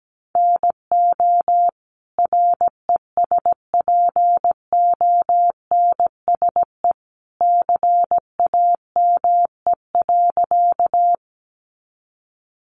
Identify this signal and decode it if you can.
Morse code, heard as a keyed tone, about 17 wpm, 700 Hz. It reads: NO RESPONSE CAME.
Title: Sentences from Top 1400 Words 17wpm